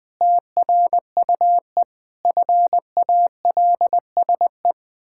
Morse code, heard as a keyed tone, about 20 wpm, 700 Hz. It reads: TRUE FALSE
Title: Pangram Challenge v2 - 20wpm